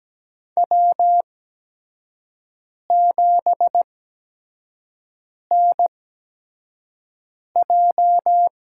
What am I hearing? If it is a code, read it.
Morse code, heard as a keyed tone, about 17 wpm, spaced as for 5 wpm, 700 Hz. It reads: W7NJ
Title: USA Call Signs - 4 Characters - 8x Character Spacing - 17wpm